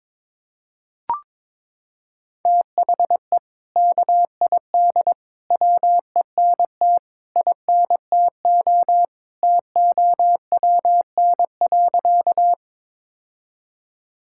Morse code, heard as a keyed tone, about 22 wpm, 700 Hz. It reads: THE KID WENT INTO TOWN.
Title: Sentences from Top 1600 Words 22wpm